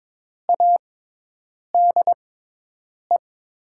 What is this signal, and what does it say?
Morse code, heard as a keyed tone, about 21 wpm, spaced as for 8 wpm, 700 Hz. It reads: ADE
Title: USA Call Signs - 6 Characters - 6x Character Spacing - 22wpm